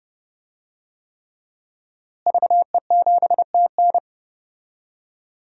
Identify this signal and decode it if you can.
Morse code, heard as a keyed tone, about 32 wpm, 700 Hz. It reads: VE7TN
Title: DXCC-100 Call Signs - 5 Characters - Repeated 3x - 30wpm